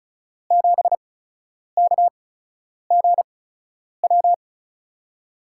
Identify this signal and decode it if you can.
Morse code, heard as a keyed tone, about 36 wpm, spaced as for 10 wpm, 700 Hz. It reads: 7KGW
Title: USA Call Signs - 6 Characters - 8x Character Spacing - 35wpm